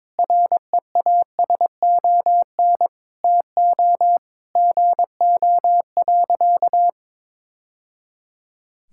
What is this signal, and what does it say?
Morse code, heard as a keyed tone, about 22 wpm, 700 Hz. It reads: REASON TO GO.
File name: practice / 22wpm / Sentences with 3-Word Phrases - 22wpm